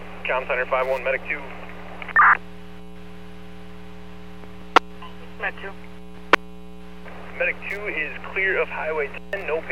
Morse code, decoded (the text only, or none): none